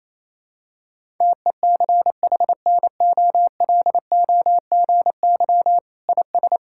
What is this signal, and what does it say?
Morse code, heard as a keyed tone, about 28 wpm, 700 Hz. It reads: TECHNOLOGY IS
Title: Sentences from Top 1000 Words - Review Part 5 - 28wpm